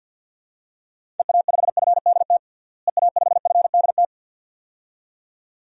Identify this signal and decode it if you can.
Morse code, heard as a keyed tone, about 52 wpm, 700 Hz. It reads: EA5FDT EA5FDT
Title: DXCC-100 Call Signs - 6 Characters - Repeated 2x - 50wpm